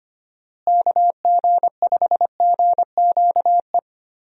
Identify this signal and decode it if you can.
Morse code, heard as a keyed tone, about 25 wpm, 700 Hz. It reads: KG5GQE
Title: USA Call Signs - 6 Characters - Repeated 3x - 25wpm